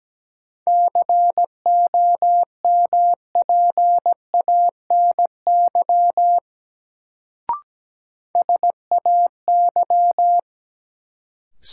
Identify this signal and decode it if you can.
Morse code, heard as a keyed tone, about 17 wpm, 700 Hz. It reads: COMPANY SAY
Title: Sets of 2 Words - New Word Drill - 17wpm